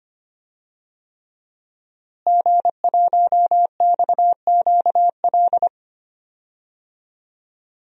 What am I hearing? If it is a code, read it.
Morse code, heard as a keyed tone, about 25 wpm, 700 Hz. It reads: G1XQL
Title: DXCC-100 Call Signs - 5 Characters - Repeated 3x - 25wpm